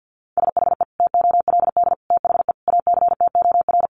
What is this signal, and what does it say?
Morse code, heard as a keyed tone, about 50 wpm, 700 Hz. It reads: SHE TOLD THE ACTOR
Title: Sentences from Top 1500 Words 50wpm